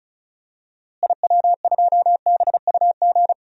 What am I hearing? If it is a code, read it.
Morse code, heard as a keyed tone, about 36 wpm, 700 Hz. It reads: IW2BUG
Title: DXCC-100 Call Signs - 6 Characters - Repeated 2x - 35wpm